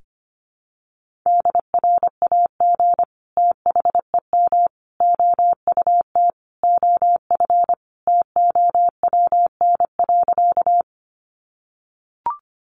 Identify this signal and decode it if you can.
Morse code, heard as a keyed tone, about 25 wpm, 700 Hz. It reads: DRAG THEM OUT OF TOWN.
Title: Sentences from Top 1500 Words 25wpm